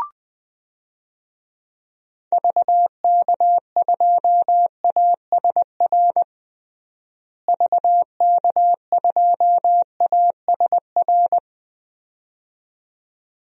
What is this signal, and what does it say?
Morse code, heard as a keyed tone, about 20 wpm, 700 Hz. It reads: VK2ASR VK2ASR
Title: DXCC-100 Call Signs - 6 Characters - Repeated 2x - 20wpm